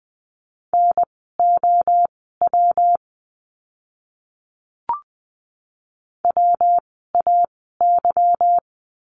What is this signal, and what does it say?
Morse code, heard as a keyed tone, about 20 wpm, spaced as for 14 wpm, 700 Hz. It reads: NOW WAY
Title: 230 Lesson 37f - Letter x - Words 20wpm